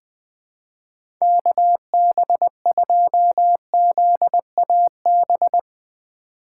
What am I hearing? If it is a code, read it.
Morse code, heard as a keyed tone, about 20 wpm, 700 Hz. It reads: KB2ZAB
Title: USA Call Signs - 6 Characters - 2x Character Spacing - 20wpm